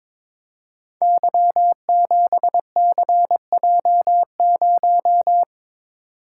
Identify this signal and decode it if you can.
Morse code, heard as a keyed tone, about 22 wpm, 700 Hz. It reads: Y7CJ0
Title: Code Groups - 5 Letter:Numbers - 22wpm